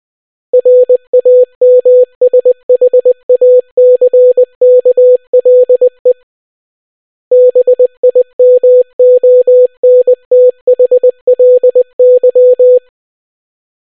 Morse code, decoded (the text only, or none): RAMSHACKLE BIMONTHLY